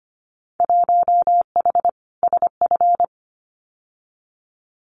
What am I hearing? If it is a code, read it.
Morse code, heard as a keyed tone, about 25 wpm, 700 Hz. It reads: 1H SF
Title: Winter Field Day Exchanges 25wpm